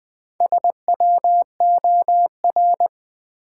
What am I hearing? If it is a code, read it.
Morse code, heard as a keyed tone, about 20 wpm, 700 Hz. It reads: SWOR
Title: Build A Word - 1 Letter At A Time - 3 to 5-Letter Words - 20wpm